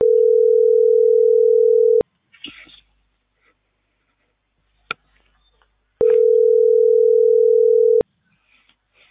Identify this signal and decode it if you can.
Ringback tone (North America), 440 + 480 Hz.